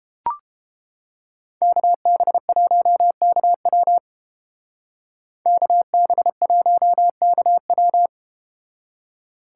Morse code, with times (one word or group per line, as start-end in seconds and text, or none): 1.6-4.0 KB1KW
5.5-8.1 KB1KW